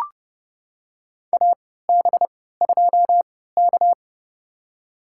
Morse code, heard as a keyed tone, about 30 wpm, spaced as for 17 wpm, 700 Hz. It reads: AB2K
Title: USA Call Signs - 4 Characters - 3x Character Spacing - 30wpm